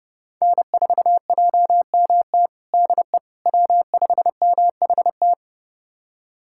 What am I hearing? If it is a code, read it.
Morse code, heard as a keyed tone, about 30 wpm, 700 Hz. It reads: N4JMT DE W5MHT
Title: USA Callsign DE Callsign - 5 Characters - 30wpm